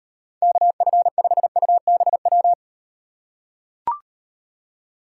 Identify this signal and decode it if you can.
Morse code, heard as a keyed tone, about 38 wpm, 700 Hz. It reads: KF5UBW